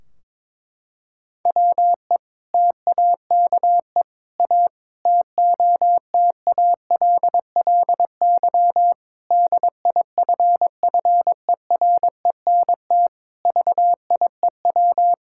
Morse code, heard as a keyed tone, about 22 wpm, 700 Hz. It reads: WE TAKE A TOTALLY DIFFERENT VIEW
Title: Sentences from Top 1600 Words 22wpm